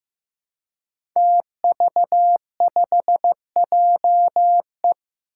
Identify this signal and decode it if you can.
Morse code, heard as a keyed tone, about 15 wpm, 700 Hz. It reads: TV5JE
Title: Five Challenging Letter-Number Combinations 15wpm